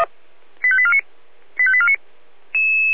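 Five-tone selective call (ZVEI), heard as quick runs of five tones, tones of about 70 ms. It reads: 85849 85849
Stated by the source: ZVEI five-tone selective call on 496, received by SDR.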